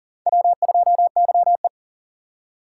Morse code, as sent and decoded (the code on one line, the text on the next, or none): .-- ..--- -.-- .
W2YE